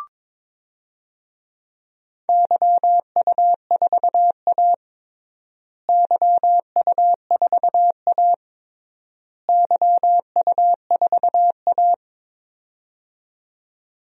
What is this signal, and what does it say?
Morse code, heard as a keyed tone, about 22 wpm, 700 Hz. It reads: YU4A YU4A YU4A
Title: DXCC-100 Call Signs - 4 Characters - Repeated 3x - 22wpm